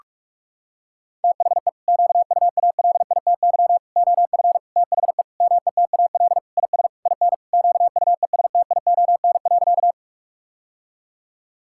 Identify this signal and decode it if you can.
Morse code, heard as a keyed tone, about 41 wpm, 700 Hz. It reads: THE QUALITY OF THE METAL IS IN QUESTION.
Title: Sentences from Top 1000 Words - Review Part 6 - 45wpm